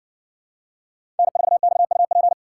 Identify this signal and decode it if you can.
Morse code, heard as a keyed tone, about 60 wpm, 700 Hz. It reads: N4XUP